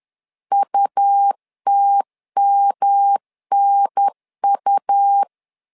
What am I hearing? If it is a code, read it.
Morse code, heard as a keyed tone, about 11 wpm, 800 Hz. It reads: UTMNU
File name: LEZIONI / 3 MWRO / 3 M 10wpm 1828